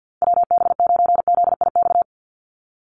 Morse code, heard as a keyed tone, about 50 wpm, 700 Hz. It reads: WB9ZIX
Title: USA Call Signs 50wpm